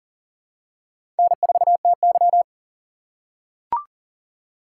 Morse code, heard as a keyed tone, about 40 wpm, 700 Hz. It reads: N4TY
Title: USA Call Signs - 4 Characters - 4x Character Spacing - 40wpm